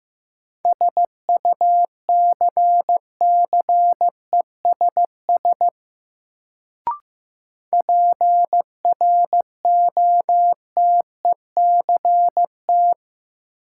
Morse code, heard as a keyed tone, about 15 wpm, 700 Hz. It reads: SUCCESS PROTECT
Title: Sentences from Top 900 Words - New Word Drill - 15wpm